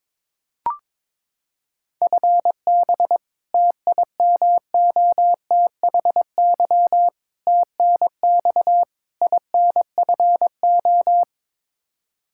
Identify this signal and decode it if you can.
Morse code, heard as a keyed tone, about 22 wpm, 700 Hz. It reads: FB TIMOTHY TNX INFO
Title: Practice QSO - Exchange 3 - More Info - 22wpm